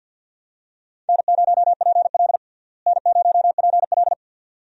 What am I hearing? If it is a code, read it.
Morse code, heard as a keyed tone, about 49 wpm, 700 Hz. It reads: N0PL N0PL
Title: USA Call Signs - 4 Characters - Repeated 2x - 50wpm